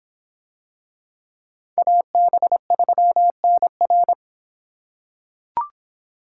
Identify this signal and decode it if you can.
Morse code, heard as a keyed tone, about 25 wpm, 700 Hz. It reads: AB3NR